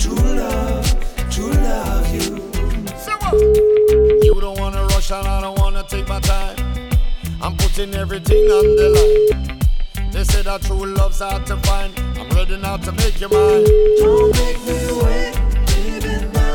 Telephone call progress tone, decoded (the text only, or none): ringback tone